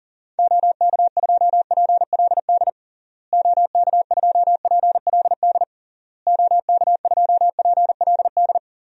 Morse code, heard as a keyed tone, about 40 wpm, 700 Hz. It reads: OK2PLD OK2PLD OK2PLD
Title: DXCC-100 Call Signs - 6 Characters - Repeated 3x - 40wpm